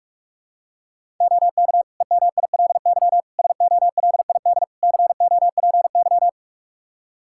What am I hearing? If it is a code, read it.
Morse code, heard as a keyed tone, about 44 wpm, 700 Hz. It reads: OK EMILY SOLID COPY
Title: Practice QSO - Exchange 3 - More Info - 45wpm